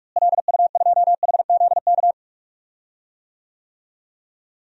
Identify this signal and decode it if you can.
Morse code, heard as a keyed tone, about 43 wpm, 700 Hz. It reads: LU2HZK
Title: DXCC-100 Call Signs - 6 Characters - Repeated 3x - 45wpm